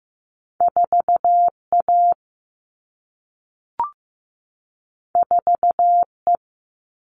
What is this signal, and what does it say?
Morse code, heard as a keyed tone, about 15 wpm, 700 Hz. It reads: 4A 4E
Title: All Cut Numbers - 2-Digit 15wpm